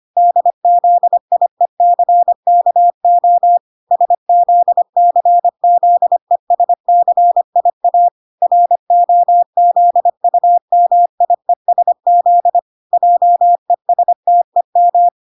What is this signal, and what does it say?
Morse code, heard as a keyed tone, about 25 wpm, 700 Hz. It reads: DZIECKO SZCZESCIA ROZUMIESZ JESTEM